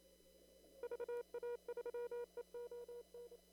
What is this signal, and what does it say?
Morse code, heard as a keyed tone, about 28 wpm, 480 Hz. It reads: VA3EON
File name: CW / kt5j wpx cw-op-k5pi